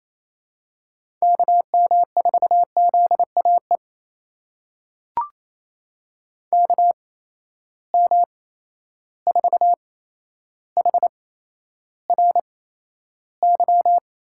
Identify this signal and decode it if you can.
Morse code, heard as a keyed tone, about 28 wpm, spaced as for 12 wpm, 700 Hz. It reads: KM4ZAE KM4HRY
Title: USA Call Signs - 6 Characters - 8x Character Spacing - 28wpm